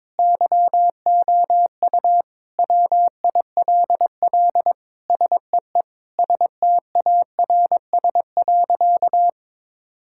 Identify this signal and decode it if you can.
Morse code, heard as a keyed tone, about 22 wpm, 700 Hz. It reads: YOU WILL SEE STARS.
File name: practice / 22wpm / Sentences with 4-Word Phrases - 22wpm